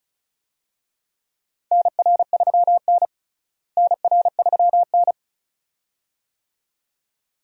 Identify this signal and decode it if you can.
Morse code, heard as a keyed tone, about 36 wpm, 700 Hz. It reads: NR3N NR3N
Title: USA Call Signs - 4 Characters - Repeated 2x - 35wpm